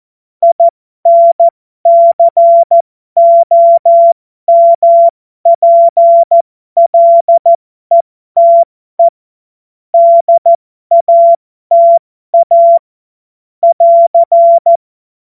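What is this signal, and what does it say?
Morse code, heard as a keyed tone, about 14 wpm, spaced as for 12 wpm, 680 Hz. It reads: INCOMPLETE DATA <AR>
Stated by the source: Morse code joke practice.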